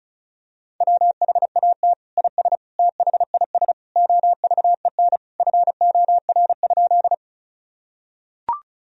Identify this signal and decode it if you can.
Morse code, heard as a keyed tone, about 36 wpm, 700 Hz. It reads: WHAT IS THIS OVEN FOR?
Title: Sentences from Top 1000 Words - Review Part 4 - 35wpm